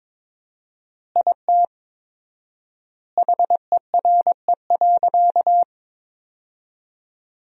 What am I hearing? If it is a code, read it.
Morse code, heard as a keyed tone, about 22 wpm, 700 Hz. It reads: IT HERE.
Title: 4x Word-Spacing - Sentences from Top 400 Words - 22wpm